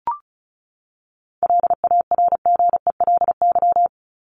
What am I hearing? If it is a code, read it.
Morse code, heard as a keyed tone, about 34 wpm, 700 Hz. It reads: LARGELY
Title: LY Suffix - Top 100 Words with Base Words 35wpm